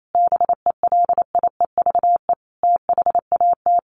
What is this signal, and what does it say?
Morse code, heard as a keyed tone, about 28 wpm, 700 Hz. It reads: BELIEVE THAT
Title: Sentences from Top 1300 Words 28wpm